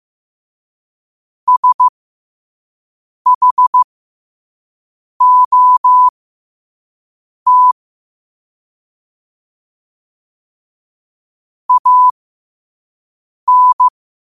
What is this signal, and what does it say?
Morse code, heard as a keyed tone, about 15 wpm, spaced as for 5 wpm, 1000 Hz. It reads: SHOT AN
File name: qotdfiles / 5wpm